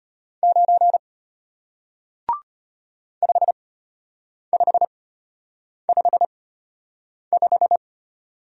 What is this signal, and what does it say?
Morse code, heard as a keyed tone, about 33 wpm, spaced as for 8 wpm, 700 Hz. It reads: 9 5555